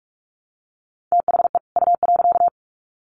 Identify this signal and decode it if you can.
Morse code, heard as a keyed tone, about 44 wpm, 700 Hz. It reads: THE U.
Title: Sets of 2 Words 45wpm